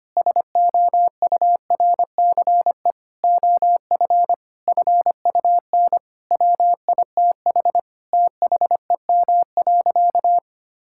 Morse code, heard as a keyed tone, about 25 wpm, 700 Hz. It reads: SOURCE OF FUN WITH THEM.
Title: Sentences from Top 1000 Words - Review Part 5 - 25wpm